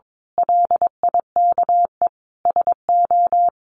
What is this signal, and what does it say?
Morse code, heard as a keyed tone, about 22 wpm, 700 Hz. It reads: LIKE SO
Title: A Sound Of Thunder 22wpm